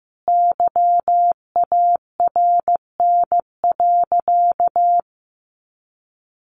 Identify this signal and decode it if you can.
Morse code, heard as a keyed tone, about 14 wpm, 700 Hz. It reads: YARN.
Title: Additional Easy Phrases 15wpm